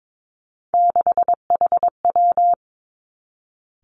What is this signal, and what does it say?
Morse code, heard as a keyed tone, about 22 wpm, 700 Hz. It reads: 6HW
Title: Three Challenging Letter-Number Combinations 22wpm